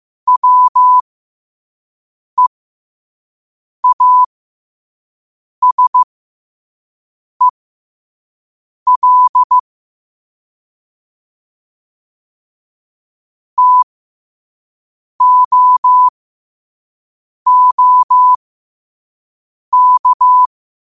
Morse code, heard as a keyed tone, about 15 wpm, spaced as for 5 wpm, 1000 Hz. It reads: WEASEL TOOK
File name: qotdfiles / 5wpm